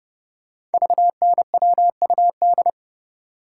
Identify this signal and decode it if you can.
Morse code, heard as a keyed tone, about 28 wpm, 700 Hz. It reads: VNWUD
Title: Code Groups - 5 Letters - 30wpm